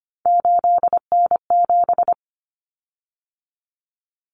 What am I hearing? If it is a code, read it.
Morse code, heard as a keyed tone, about 25 wpm, 700 Hz. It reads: 8N7
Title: Common Cut Numbers - 3-Digit 25wpm